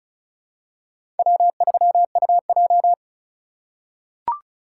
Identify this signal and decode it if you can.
Morse code, heard as a keyed tone, about 36 wpm, 700 Hz. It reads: W3UJ
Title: USA Call Signs - 4 Characters - 2x Character Spacing - 35wpm